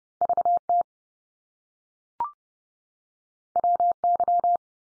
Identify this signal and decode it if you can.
Morse code, heard as a keyed tone, about 30 wpm, 700 Hz. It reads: VT WY
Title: US States 30WPM